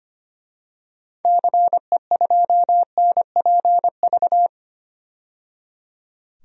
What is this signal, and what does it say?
Morse code, heard as a keyed tone, about 25 wpm, 700 Hz. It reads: CE2NPV
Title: DXCC-100 Call Signs - 6 Characters - 25wpm